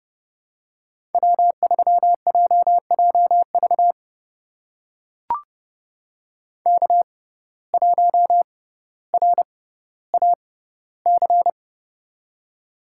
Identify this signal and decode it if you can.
Morse code, heard as a keyed tone, about 29 wpm, spaced as for 15 wpm, 700 Hz. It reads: W3JJV K1RAC
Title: USA Call Signs - 5 Characters - 6x Character Spacing - 30wpm